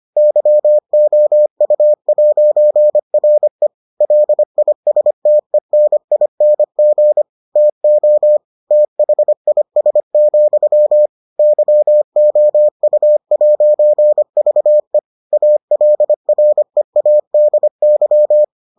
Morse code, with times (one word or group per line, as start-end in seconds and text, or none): 0.2-3.7 YOU'RE
4.0-7.2 LISTENING
7.6-8.4 TO
8.7-11.1 THIS,
11.4-15.0 YOU'VE
15.3-18.4 ALREADY